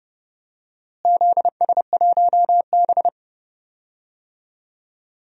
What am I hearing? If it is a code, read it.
Morse code, heard as a keyed tone, about 30 wpm, 700 Hz. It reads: ZS1B
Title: DXCC-100 Call Signs - 4 Characters - Repeated 2x - 30wpm